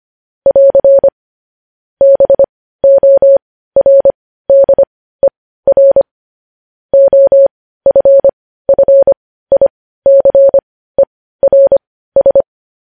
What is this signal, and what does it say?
Morse code, heard as a keyed tone, about 25 wpm, spaced as for 15 wpm, 550 Hz. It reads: <AR> BORDER OFFICERS
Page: News headlines at 15